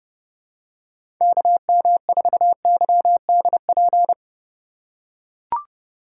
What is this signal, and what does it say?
Morse code, heard as a keyed tone, about 31 wpm, 700 Hz. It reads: KM4YDP
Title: USA Call Signs - 6 Characters - 2x Character Spacing - 30wpm